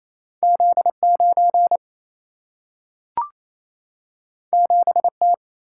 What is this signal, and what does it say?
Morse code, heard as a keyed tone, about 28 wpm, 700 Hz. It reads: Z9 7T
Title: Single-Letter Single-Number 28wpm